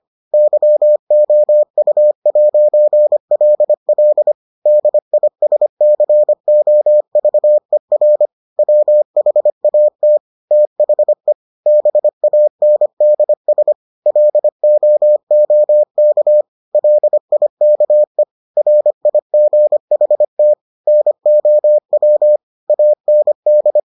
Morse code, heard as a keyed tone, about 25 wpm, 600 Hz. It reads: YOU'LL DISCOVER WHAT THE BANDS LOOK LIKE RIGHT NOW AND